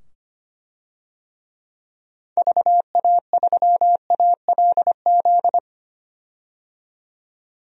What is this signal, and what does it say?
Morse code, heard as a keyed tone, about 25 wpm, 690 Hz. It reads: VA3ALZ
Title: DXCC-100 Call Signs - 6 Characters - 25wpm